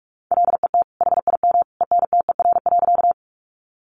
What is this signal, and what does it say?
Morse code, heard as a keyed tone, about 44 wpm, 700 Hz. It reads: LET HIM ENTER.
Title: Sentences from Top 1300 Words 45wpm